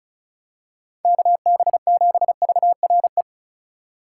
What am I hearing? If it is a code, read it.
Morse code, heard as a keyed tone, about 34 wpm, 700 Hz. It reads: KB7VRE
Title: USA Call Signs - 6 Characters - 8x Character Spacing - 35wpm